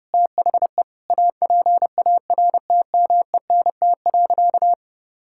Morse code, heard as a keyed tone, about 32 wpm, 700 Hz. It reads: THE APARTMENT.
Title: Sentences from Top 1000 Words - Review Part 6 - 30wpm